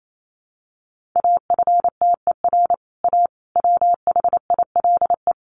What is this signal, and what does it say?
Morse code, heard as a keyed tone, about 28 wpm, 700 Hz. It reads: AFTER A WHILE
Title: Sentences from Top 1000 Words 28wpm